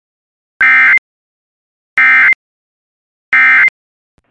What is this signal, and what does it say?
Emergency Alert System (SAME) tones, heard as harsh buzzing bursts. It is heard from 0.6 to 3.7 s.